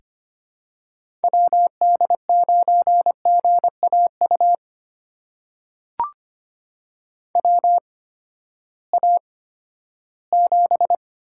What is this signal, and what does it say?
Morse code, heard as a keyed tone, about 25 wpm, spaced as for 14 wpm, 700 Hz. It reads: WD9GAU WA7
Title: USA Call Signs - 6 Characters - 8x Character Spacing - 25wpm